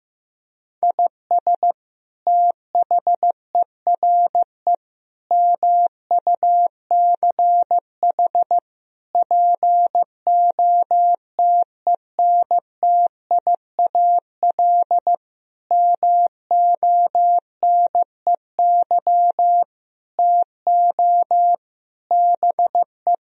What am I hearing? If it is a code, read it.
Morse code, heard as a keyed tone, about 15 wpm, 700 Hz. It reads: IS THERE MUCH POTENTIAL MONEY TO BE
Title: Sentences from Top 1000 Words - Review Part 4 - 15wpm